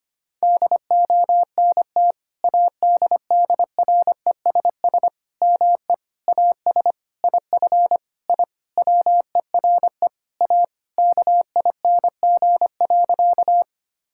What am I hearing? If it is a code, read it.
Morse code, heard as a keyed tone, about 25 wpm, 700 Hz. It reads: DONT ADDRESS ME AS IF I WERE A KING.
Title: Sentences from Top 1000 Words - Review Part 6 - 25wpm